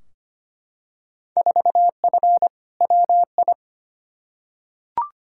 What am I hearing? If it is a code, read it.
Morse code, heard as a keyed tone, about 25 wpm, 700 Hz. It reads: 4F WI
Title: Field Day Exchanges Encore - 25wpm